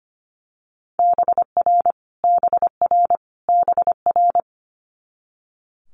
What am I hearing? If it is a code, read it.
Morse code, heard as a keyed tone, about 25 wpm, 700 Hz. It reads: BR BR BR
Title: Multi-letter Phonemes Repeated 3x 25wpm